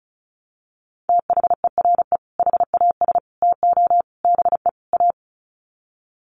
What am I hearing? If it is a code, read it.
Morse code, heard as a keyed tone, about 34 wpm, 700 Hz. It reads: THERE HAS TO BE A
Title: Sets of 5 Words Encore 35wpm